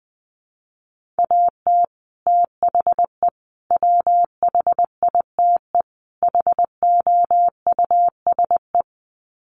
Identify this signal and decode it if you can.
Morse code, heard as a keyed tone, about 20 wpm, 700 Hz. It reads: AT THE WHITE HOUSE
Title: Sets of 4 Words 20wpm